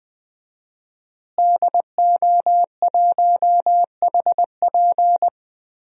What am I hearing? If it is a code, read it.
Morse code, heard as a keyed tone, about 20 wpm, 700 Hz. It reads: DO1HP
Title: DXCC-100 Call Signs - 5 Characters - Repeated 2x - 20wpm